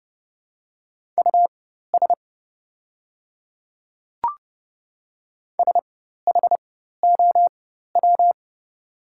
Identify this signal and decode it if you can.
Morse code, heard as a keyed tone, about 31 wpm, spaced as for 14 wpm, 700 Hz. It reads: US SHOW
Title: 4x Character Spacing - Top 200 Words - 30wpm